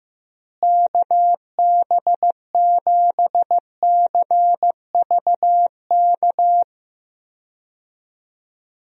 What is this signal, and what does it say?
Morse code, heard as a keyed tone, about 15 wpm, 700 Hz. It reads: KB7CVK
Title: USA Call Signs - 6 Characters - Repeated 2x - 15wpm